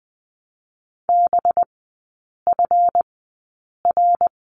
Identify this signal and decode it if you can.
Morse code, heard as a keyed tone, about 20 wpm, spaced as for 8 wpm, 700 Hz. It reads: BFR
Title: 184 Lesson 30e - Letter b - Three Characters 20wpm